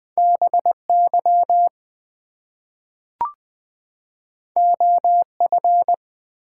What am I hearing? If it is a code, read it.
Morse code, heard as a keyed tone, about 20 wpm, 700 Hz. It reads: BY OF
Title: Two Letter Words - 20wpm